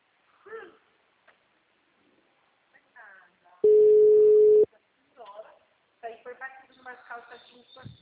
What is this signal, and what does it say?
Ringback tone (Europe, 425 Hz), with speech.
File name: out- -2008- - - .60394